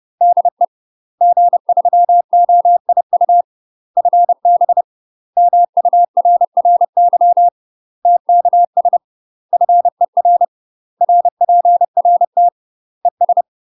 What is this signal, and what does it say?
Morse code, heard as a keyed tone, about 30 wpm, 700 Hz. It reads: DE G3OIU FB MURRY TKS FER RPRT ES
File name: UK MASTER QSOs TRAINING / 30WPM / UK MASTER QSOs 0045